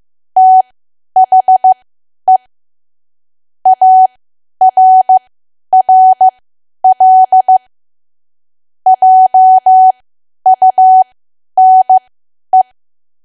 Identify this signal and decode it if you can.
Morse code, heard as a keyed tone, about 15 wpm, spaced as for 10 wpm, 750 Hz. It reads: THE ARRL JUNE
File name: Archive / 10 WPM / 10WPM